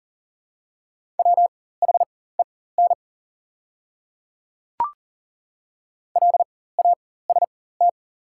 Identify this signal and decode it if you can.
Morse code, heard as a keyed tone, about 40 wpm, spaced as for 19 wpm, 700 Hz. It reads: WHEN LAST
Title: 4x Character Spacing - Top 500 Words - 40wpm